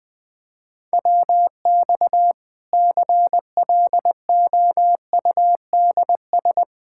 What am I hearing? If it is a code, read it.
Morse code, heard as a keyed tone, about 21 wpm, 700 Hz. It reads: WX CLOUDS
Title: Practice QSO - Exchange 2 - Info - 20wpm